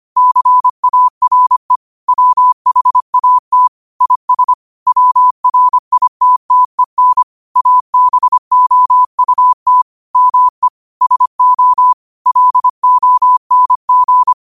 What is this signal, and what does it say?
Morse code, heard as a keyed tone, about 25 wpm, 1000 Hz. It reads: CARE WHAT IS WRITTEN ABOUT ME SO LONG
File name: qotdfiles / 25wpm